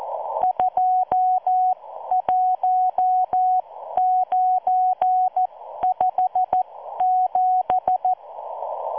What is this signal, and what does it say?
Morse code, heard as a keyed tone, about 14 wpm, spaced as for 12 wpm, 740 Hz. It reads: 21957